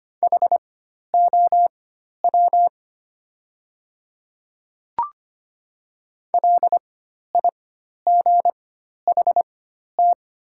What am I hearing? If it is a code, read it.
Morse code, heard as a keyed tone, about 25 wpm, spaced as for 12 wpm, 700 Hz. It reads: HOW LIGHT
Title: 4x Character Spacing - Top 300 Words - 25wpm